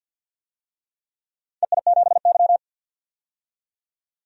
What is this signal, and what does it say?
Morse code, heard as a keyed tone, about 47 wpm, 700 Hz. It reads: EI7Y
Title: DXCC-100 Call Signs - 4 Characters - 50wpm